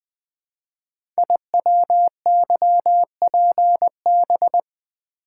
Morse code, heard as a keyed tone, about 20 wpm, 700 Hz. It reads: IWYPB
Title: Code Groups - 5 Letters - 20wpm